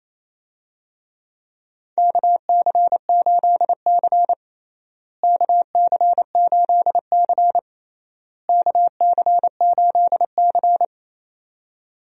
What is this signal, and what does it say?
Morse code, heard as a keyed tone, about 27 wpm, 700 Hz. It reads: KC8C KC8C KC8C
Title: USA Call Signs - 4 Characters - Repeated 3x - 28wpm